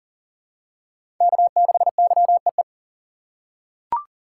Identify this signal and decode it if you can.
Morse code, heard as a keyed tone, about 40 wpm, 700 Hz. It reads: K6YEE
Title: USA Call Signs - 5 Characters - 3x Character Spacing - 40wpm